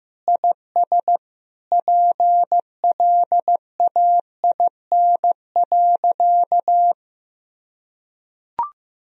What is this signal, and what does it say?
Morse code, heard as a keyed tone, about 15 wpm, 700 Hz. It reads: IS PLAIN.
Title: Sentences from Top 1000 Words - Review Part 2 - 15wpm